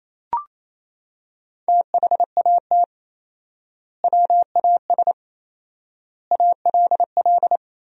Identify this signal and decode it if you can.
Morse code, heard as a keyed tone, about 28 wpm, 700 Hz. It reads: THAT WAS ALL
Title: 4x Word-Spacing - Sentences from Top 100 Words - 28wpm